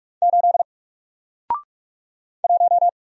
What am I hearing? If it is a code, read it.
Morse code, heard as a keyed tone, about 46 wpm, 700 Hz. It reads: 8 1
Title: Single Letter-Number Alt 45wpm